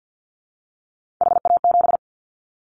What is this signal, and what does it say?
Morse code, heard as a keyed tone, about 50 wpm, 700 Hz. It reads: HA7
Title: Callsign Prefixes for Top 25 DXCC - 50wpm